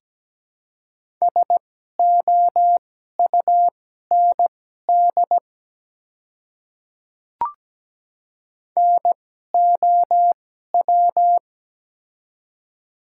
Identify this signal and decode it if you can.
Morse code, heard as a keyed tone, about 17 wpm, spaced as for 12 wpm, 700 Hz. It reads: SOUND NOW
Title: 2x Character Spacing - Top 100 Words - 17wpm